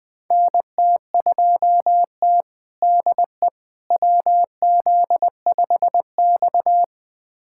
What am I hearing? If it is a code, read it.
Morse code, heard as a keyed tone, about 20 wpm, 700 Hz. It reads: NT2T DE WZ5X
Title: USA Callsign DE Callsign - 4 Characters - 20wpm